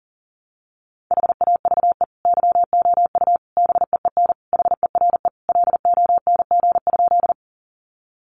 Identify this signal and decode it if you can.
Morse code, heard as a keyed tone, about 37 wpm, 700 Hz. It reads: HAVE YOU BEEN HERE LONG?
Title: Sentences from Top 200 Words 40wpm